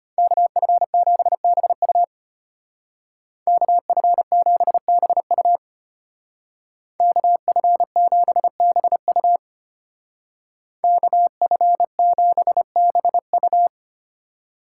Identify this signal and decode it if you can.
Morse code, heard as a keyed tone, about 30 wpm, 700 Hz. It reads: KF7BU KF7BU KF7BU KF7BU
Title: Speed Racing - USA 2x2 Call Signs 25wpm